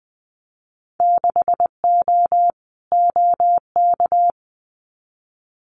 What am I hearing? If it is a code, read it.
Morse code, heard as a keyed tone, about 20 wpm, 700 Hz. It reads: 6O OK
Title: Winter Field Day Exchanges Encore 20wpm